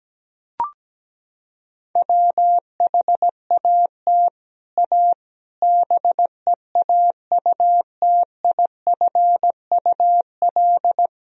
Morse code, heard as a keyed tone, about 17 wpm, 700 Hz. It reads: WHAT A BEAUTIFUL
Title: Sentences from Top 1000 Words - Review Part 1 - 17wpm